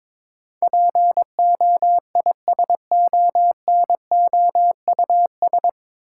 Morse code, heard as a keyed tone, about 22 wpm, 700 Hz. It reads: POISONOUS
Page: Morse Code Ninja practice 22wpm